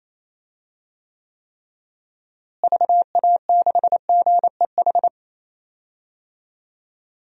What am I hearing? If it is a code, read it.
Morse code, heard as a keyed tone, about 28 wpm, 700 Hz. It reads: VA6GEH
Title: DXCC-100 Call Signs - 6 Characters - Repeated 3x - 28wpm